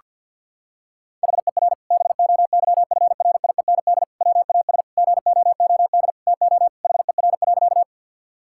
Morse code, heard as a keyed tone, about 50 wpm, 700 Hz. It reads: HER BOYFRIEND WAS GOOD TO HER.
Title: Sentences from Top 1000 Words - Review Part 5 - 50wpm